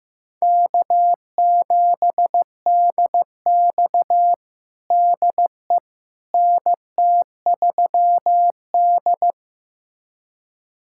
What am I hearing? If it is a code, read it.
Morse code, heard as a keyed tone, about 15 wpm, 700 Hz. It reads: K7DX DE NT3D
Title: USA Callsign DE Callsign - 4 Characters - 15wpm